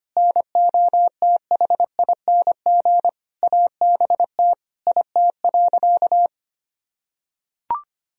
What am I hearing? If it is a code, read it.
Morse code, heard as a keyed tone, about 25 wpm, 700 Hz. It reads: NOTHING ABT IT.
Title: Sentences from Top 500 Words and Abbreviations 25wpm